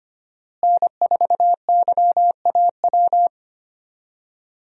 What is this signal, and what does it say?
Morse code, heard as a keyed tone, about 25 wpm, 700 Hz. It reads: N4YAW
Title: USA Call Signs - 5 Characters - Repeated 3x - 25wpm